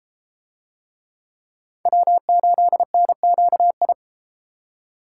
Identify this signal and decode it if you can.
Morse code, heard as a keyed tone, about 33 wpm, 700 Hz. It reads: W8NQI